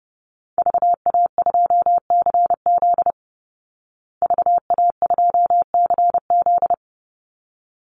Morse code, heard as a keyed tone, about 30 wpm, 700 Hz. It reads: VA2CZ VA2CZ
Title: Call Signs Repeated 3x - 30wpm